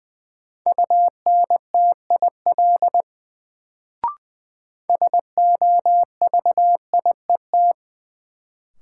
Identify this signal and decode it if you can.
Morse code, heard as a keyed tone, about 20 wpm, 700 Hz. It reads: UNTIL SOVIET
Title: Sets of 5 Words Encore - New Word Drill - 20wpm